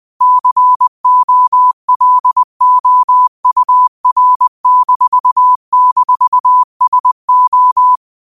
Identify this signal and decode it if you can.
Morse code, heard as a keyed tone, about 20 wpm, 1000 Hz. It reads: COLOUR--SO
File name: qotdfiles / 20wpm